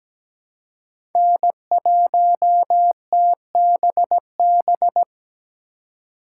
Morse code, heard as a keyed tone, about 17 wpm, 700 Hz. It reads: N1TBB
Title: USA Call Signs - 5 Characters - 2x Character Spacing - 17wpm